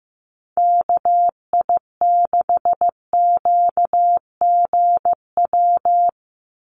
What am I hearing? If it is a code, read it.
Morse code, heard as a keyed tone, about 15 wpm, 700 Hz. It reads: KI6QGW
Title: USA Call Signs 15wpm